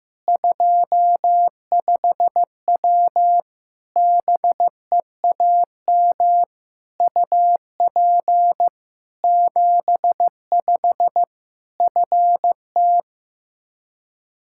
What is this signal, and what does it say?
Morse code, heard as a keyed tone, about 15 wpm, 700 Hz. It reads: 25W BEAM UP 75 FT
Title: Practice QSO - Exchange 2 - Info - 15wpm